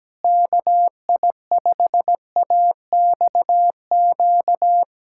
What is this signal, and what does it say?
Morse code, heard as a keyed tone, about 17 wpm, 700 Hz. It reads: KI5AXQ
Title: USA Callsign DE Callsign - 6 Characters - 17wpm